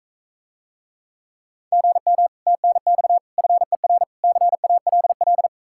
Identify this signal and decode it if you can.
Morse code, heard as a keyed tone, about 44 wpm, 700 Hz. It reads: GM TNX FER CALL